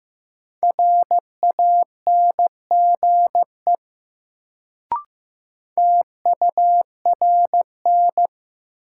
Morse code, heard as a keyed tone, about 15 wpm, 700 Hz. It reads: RANGE TURN
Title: Sets of 4 Words - New Word Drill - 15wpm